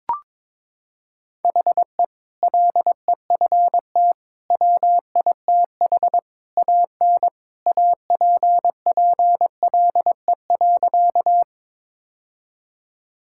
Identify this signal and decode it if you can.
Morse code, heard as a keyed tone, about 22 wpm, 700 Hz. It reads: HE LEFT WITH AN APPLE.
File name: practice / 22wpm / Sentences with 2-Word Phrases - 22wpm